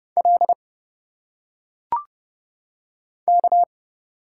Morse code, heard as a keyed tone, about 30 wpm, 700 Hz. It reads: L K
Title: Single Letters - Phonetic Pronunciation 30wpm